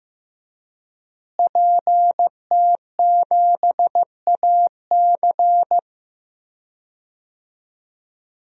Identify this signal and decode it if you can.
Morse code, heard as a keyed tone, about 15 wpm, 700 Hz. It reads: PT7AC
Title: DXCC-100 Call Signs - 5 Characters - Repeated 3x - 15wpm